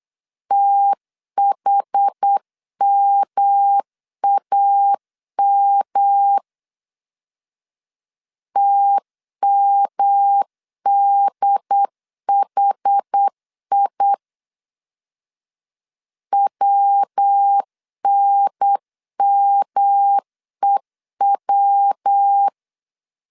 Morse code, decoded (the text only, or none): THMAM TMDHI WNMEW